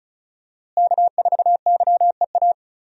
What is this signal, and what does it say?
Morse code, heard as a keyed tone, about 34 wpm, 700 Hz. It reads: K4YEA
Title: USA Call Signs - 5 Characters - Repeated 3x - 35wpm